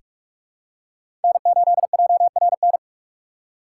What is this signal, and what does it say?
Morse code, heard as a keyed tone, about 43 wpm, 700 Hz. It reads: N8JRN
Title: USA Call Signs - 5 Characters - 2x Character Spacing - 45wpm